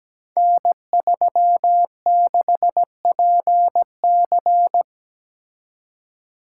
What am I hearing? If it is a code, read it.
Morse code, heard as a keyed tone, about 17 wpm, 700 Hz. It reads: N36PC
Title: Code Groups - 5 Letter:Numbers - 17wpm